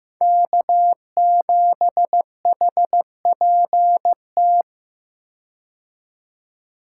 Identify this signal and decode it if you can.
Morse code, heard as a keyed tone, about 16 wpm, 700 Hz. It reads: K7HPT
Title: K1USN SST QSO Practice - 15wpm